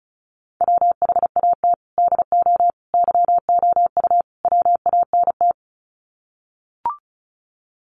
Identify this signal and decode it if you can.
Morse code, heard as a keyed tone, about 34 wpm, 700 Hz. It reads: WHAT DO YOU WANT
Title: Sets of 4 Words 35wpm